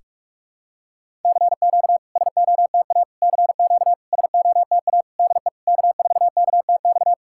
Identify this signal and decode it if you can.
Morse code, heard as a keyed tone, about 46 wpm, 690 Hz. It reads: CQ SOTA CQ SOTA DE K4KTX
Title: SOTA QSO Practice 45wpm